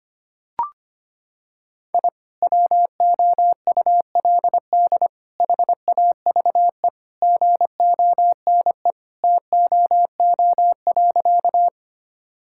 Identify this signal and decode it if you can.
Morse code, heard as a keyed tone, about 25 wpm, 700 Hz. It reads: I WOULD HAVE GONE TOO.
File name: practice / 25wpm / Sentences with 2-Word Phrases - 25wpm